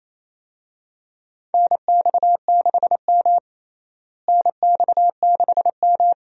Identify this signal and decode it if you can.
Morse code, heard as a keyed tone, about 28 wpm, 700 Hz. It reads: NX6M NX6M
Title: USA Call Signs - 4 Characters - Repeated 2x - 28wpm